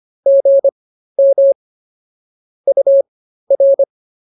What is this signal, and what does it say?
Morse code, heard as a keyed tone, about 24 wpm, spaced as for 13 wpm, 550 Hz. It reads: GM UR